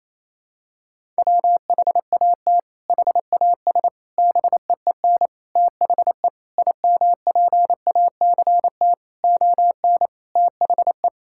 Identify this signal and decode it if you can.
Morse code, heard as a keyed tone, about 27 wpm, 700 Hz. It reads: WHAT HAS BEEN THE IMPACT ON THE
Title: Sentences from Top 1000 Words - Review Part 6 - 28wpm